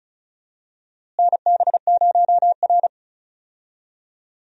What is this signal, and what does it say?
Morse code, heard as a keyed tone, about 36 wpm, 700 Hz. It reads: NB0R
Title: USA Call Signs - 4 Characters - Repeated 2x - 35wpm